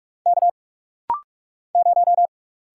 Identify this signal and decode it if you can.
Morse code, heard as a keyed tone, about 44 wpm, 700 Hz. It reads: K 0
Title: Single Letter-Number - Rapid-Fire 45wpm